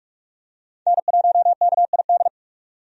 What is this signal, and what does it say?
Morse code, heard as a keyed tone, about 43 wpm, 700 Hz. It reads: N1KID
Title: USA Call Signs - 5 Characters - 45wpm